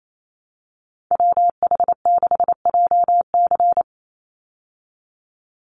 Morse code, heard as a keyed tone, about 28 wpm, 700 Hz. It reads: WH6JC
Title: USA Call Signs 28wpm